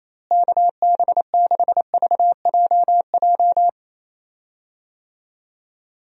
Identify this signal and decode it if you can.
Morse code, heard as a keyed tone, about 28 wpm, 700 Hz. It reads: KB6VJJ
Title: USA Call Signs - 6 Characters - Repeated 3x - 28wpm